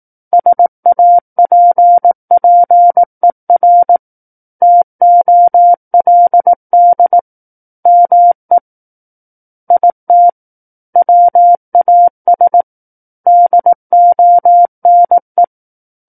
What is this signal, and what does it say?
Morse code, heard as a keyed tone, about 18 wpm, 700 Hz. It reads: SAPPER TOLD ME IT WAS DONE
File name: Book One / 09-Chapter 9 - 18 WPM